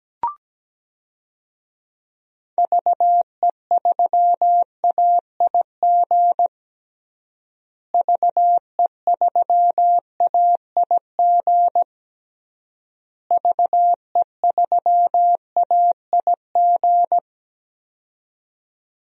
Morse code, heard as a keyed tone, about 17 wpm, 700 Hz. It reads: VE3AIG VE3AIG VE3AIG
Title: DXCC-100 Call Signs - 6 Characters - Repeated 3x - 17wpm